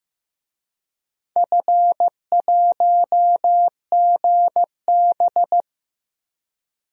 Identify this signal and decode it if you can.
Morse code, heard as a keyed tone, about 15 wpm, 700 Hz. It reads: F1GB
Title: DXCC-100 Call Signs - 4 Characters - Repeated 2x - 15wpm